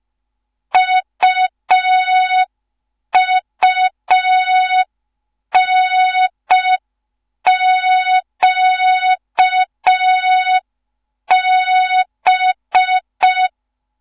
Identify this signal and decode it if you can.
Morse code, heard as a keyed tone, about 5 wpm, 3700 Hz. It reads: UUNQB
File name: Morze-HA2MN / hang / 15 GYAKORLAT